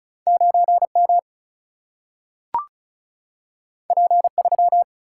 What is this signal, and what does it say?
Morse code, heard as a keyed tone, about 36 wpm, 700 Hz. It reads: 9M P3
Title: Single-Letter Single-Number 35wpm